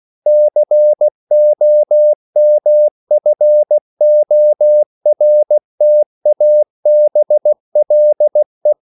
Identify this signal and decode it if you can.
Morse code, heard as a keyed tone, about 16 wpm, 600 Hz. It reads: COMFORTABLE